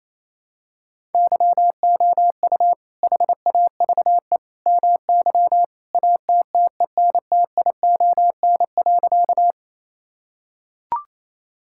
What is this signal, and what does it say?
Morse code, heard as a keyed tone, about 27 wpm, 700 Hz. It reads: YOU HAVE MY ATTENTION.
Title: Sentences from Top 1000 Words - Review Part 6 - 28wpm